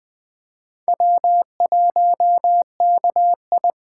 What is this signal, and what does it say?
Morse code, heard as a keyed tone, about 20 wpm, 700 Hz. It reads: W1KI
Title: Speed Racing - USA 1x2 Call Signs 17wpm